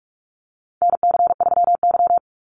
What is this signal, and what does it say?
Morse code, heard as a keyed tone, about 44 wpm, 700 Hz. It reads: NC3Y
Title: Call Signs Repeated 2x - 45wpm